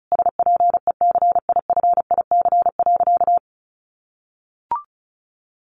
Morse code, heard as a keyed tone, about 34 wpm, 700 Hz. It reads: SPECIFIC.
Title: Sentences from Top 1300 Words 35wpm